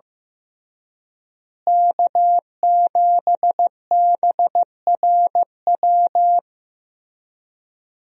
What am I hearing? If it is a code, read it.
Morse code, heard as a keyed tone, about 16 wpm, 700 Hz. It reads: K7BRW
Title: USA Call Signs - 5 Characters - Repeated 3x - 15wpm